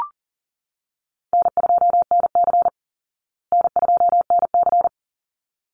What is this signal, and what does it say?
Morse code, heard as a keyed tone, about 40 wpm, 700 Hz. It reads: N2NC N2NC
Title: Call Signs Repeated 3x - 40wpm